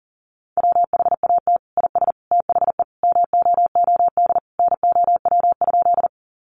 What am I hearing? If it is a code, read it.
Morse code, heard as a keyed tone, about 37 wpm, 700 Hz. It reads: WHAT IS THE MOOD NOW?
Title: Sentences from Top 1100 Words 40wpm